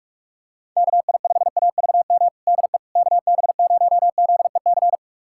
Morse code, heard as a keyed tone, about 46 wpm, 700 Hz. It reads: KI5AVM DE KB0ZEC
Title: USA Callsign DE Callsign - 6 Characters - 45wpm